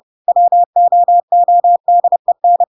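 Morse code, heard as a keyed tone, about 30 wpm, 700 Hz. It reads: WOODEN
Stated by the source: Chapter 2 of The Adventures of Pinocchio read as a book in Morse code (CW) at 30 wpm, by ik1hge.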